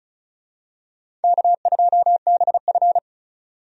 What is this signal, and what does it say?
Morse code, heard as a keyed tone, about 34 wpm, 700 Hz. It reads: K2BF
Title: USA Call Signs - 4 Characters - 35wpm